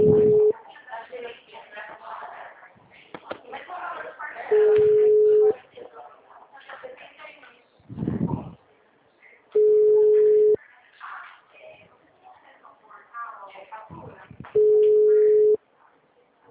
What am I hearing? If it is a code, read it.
Ringback tone (Europe, 425 Hz), with speech.
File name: out- -2003- - - .50236